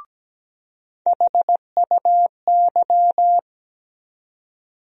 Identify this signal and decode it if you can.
Morse code, heard as a keyed tone, about 17 wpm, 700 Hz. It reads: HUY
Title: USA Call Sign Postfixes - 3 Letters 17wpm